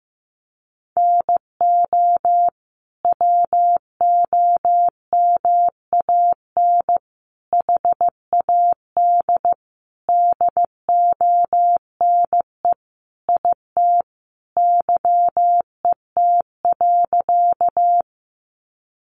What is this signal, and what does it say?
Morse code, heard as a keyed tone, about 14 wpm, 700 Hz. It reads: NO WOMAN HAD DONE IT YET.
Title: Sentences from Top 1100 Words 15wpm